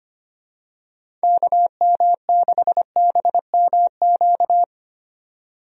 Morse code, heard as a keyed tone, about 25 wpm, 700 Hz. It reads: KM6BMQ
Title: USA Call Signs - 6 Characters - 25wpm